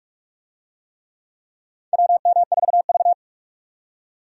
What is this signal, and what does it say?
Morse code, heard as a keyed tone, about 46 wpm, 700 Hz. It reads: WM4V
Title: USA Call Signs - 4 Characters - 45wpm